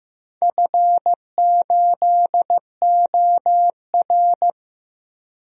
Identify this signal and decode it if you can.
Morse code, heard as a keyed tone, about 15 wpm, 700 Hz. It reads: F8OR
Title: DXCC-100 Call Signs - 4 Characters - 15wpm